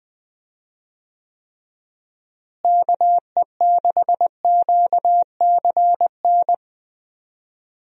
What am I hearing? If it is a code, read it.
Morse code, heard as a keyed tone, about 20 wpm, 700 Hz. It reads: KE6QCN
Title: USA Call Signs - 6 Characters - Repeated 2x - 20wpm